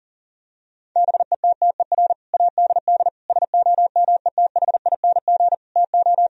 Morse code, heard as a keyed tone, about 40 wpm, 700 Hz. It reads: BETTER ADD SOMETHING TO
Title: Sentences from Top 1000 Words - Review Part 1 - 40wpm